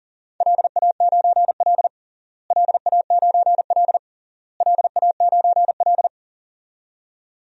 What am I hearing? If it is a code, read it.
Morse code, heard as a keyed tone, about 40 wpm, 700 Hz. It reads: LA9L LA9L LA9L
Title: DXCC-100 Call Signs - 4 Characters - Repeated 3x - 40wpm